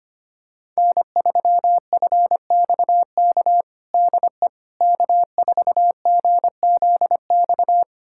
Morse code, heard as a keyed tone, about 25 wpm, 700 Hz. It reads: N3FXK DE K4GZX
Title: USA Callsign DE Callsign - 5 Characters - 25wpm